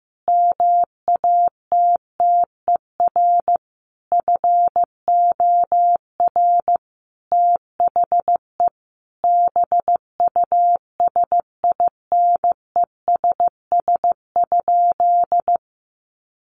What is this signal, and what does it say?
Morse code, heard as a keyed tone, about 15 wpm, 700 Hz. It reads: MATTER FOR THE BUSINESS?
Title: Sentences from Top 1400 Words 15wpm